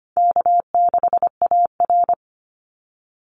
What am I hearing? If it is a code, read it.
Morse code, heard as a keyed tone, about 25 wpm, 700 Hz. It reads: K6AR
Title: Call Signs - 25wpm